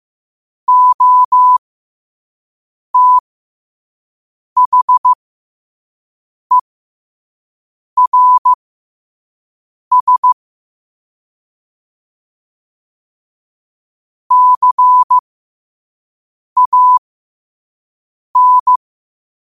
Morse code, heard as a keyed tone, about 15 wpm, spaced as for 5 wpm, 1000 Hz. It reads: OTHERS CAN
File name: 5wpm